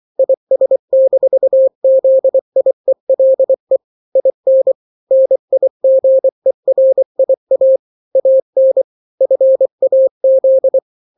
Morse code, heard as a keyed tone, about 24 wpm, 530 Hz. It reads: IS-ZIELE IN NIGERIA AN FAZ
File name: daily / 23wpm